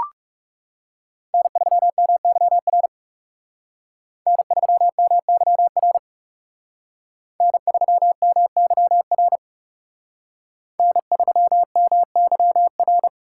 Morse code, heard as a keyed tone, about 38 wpm, 700 Hz. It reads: N3MYR N3MYR N3MYR N3MYR